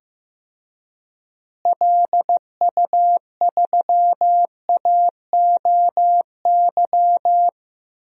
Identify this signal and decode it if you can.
Morse code, heard as a keyed tone, about 15 wpm, 700 Hz. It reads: LU3AOY
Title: DXCC-100 Call Signs - 6 Characters - Repeated 3x - 15wpm